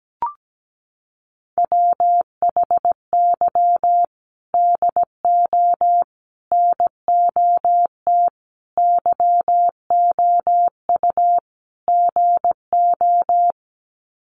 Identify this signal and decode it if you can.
Morse code, heard as a keyed tone, about 17 wpm, 700 Hz. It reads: WHY DO NOT YOU GO
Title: Sets of 5 Words Encore 17wpm